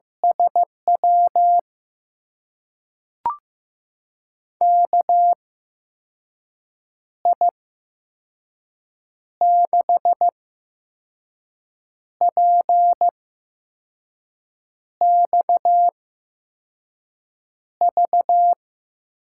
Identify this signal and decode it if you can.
Morse code, heard as a keyed tone, about 15 wpm, spaced as for 5 wpm, 700 Hz. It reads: SW KI6PXV
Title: USA Call Signs - 6 Characters - 8x Character Spacing - 15wpm